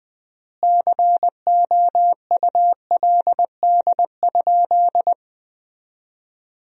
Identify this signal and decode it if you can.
Morse code, heard as a keyed tone, about 20 wpm, 700 Hz. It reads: COULD?
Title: 3x Word-Spacing - Sentences from Top 400 Words - 20wpm